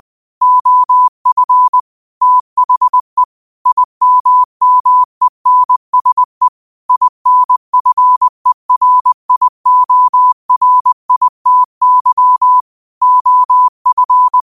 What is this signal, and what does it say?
Morse code, heard as a keyed tone, about 20 wpm, 1000 Hz. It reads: OF THE IMMENSE INFERIORITY OF
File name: qotdfiles / 20wpm